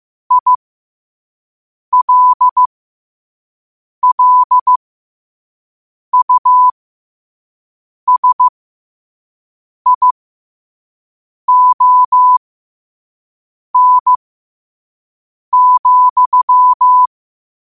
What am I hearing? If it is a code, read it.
Morse code, heard as a keyed tone, about 15 wpm, spaced as for 5 wpm, 1000 Hz. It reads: ILLUSION,
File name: qotdfiles / 5wpm